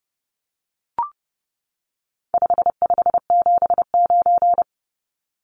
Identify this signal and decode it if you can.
Morse code, heard as a keyed tone, about 30 wpm, 700 Hz. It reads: 5579
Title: Numbers - 4-Digit 30wpm